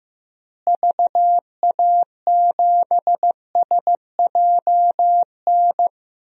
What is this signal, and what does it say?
Morse code, heard as a keyed tone, about 15 wpm, 700 Hz. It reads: VA7SJN
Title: DXCC-100 Call Signs - 6 Characters - Repeated 3x - 15wpm